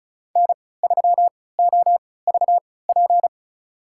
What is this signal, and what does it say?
Morse code, heard as a keyed tone, about 36 wpm, spaced as for 20 wpm, 690 Hz. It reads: N3OVP
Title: USA Call Signs - 5 Characters - 3x Character Spacing - 35wpm